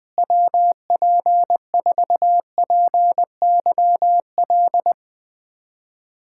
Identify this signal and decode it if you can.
Morse code, heard as a keyed tone, about 20 wpm, 700 Hz. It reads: WP4PYL
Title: USA Callsign DE Callsign - 6 Characters - 20wpm